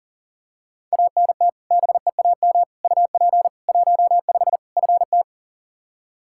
Morse code, heard as a keyed tone, about 40 wpm, 700 Hz. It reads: ANT BEAM UP 15 FT
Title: Practice QSO - Exchange 2 - Info - 40wpm